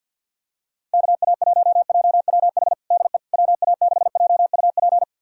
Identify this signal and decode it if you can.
Morse code, heard as a keyed tone, about 52 wpm, 690 Hz. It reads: KA1JWH DE WA6JUP
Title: USA Callsign DE Callsign - 6 Characters - 50wpm